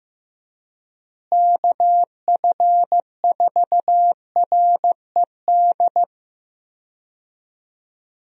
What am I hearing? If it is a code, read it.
Morse code, heard as a keyed tone, about 15 wpm, 700 Hz. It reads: KF4RED
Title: USA Call Signs - 6 Characters - Repeated 3x - 15wpm